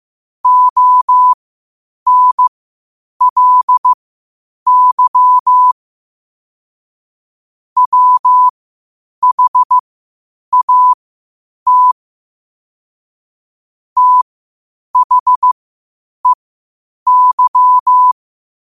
Morse code, heard as a keyed tone, about 15 wpm, spaced as for 8 wpm, 1000 Hz. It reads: ONLY WHAT THEY
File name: qotdfiles / 8wpm